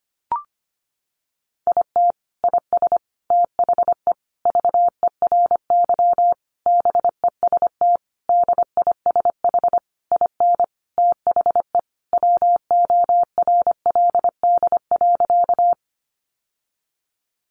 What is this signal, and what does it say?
Morse code, heard as a keyed tone, about 25 wpm, 700 Hz. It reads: IT IS THE VERY BEST DISH IN THE WORLD.
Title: Sentences from Top 1300 Words 25wpm